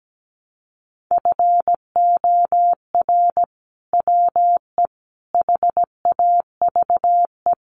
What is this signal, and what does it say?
Morse code, heard as a keyed tone, about 17 wpm, 700 Hz. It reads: FOR WE HAVE
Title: Sentences from Top 1000 Words 17wpm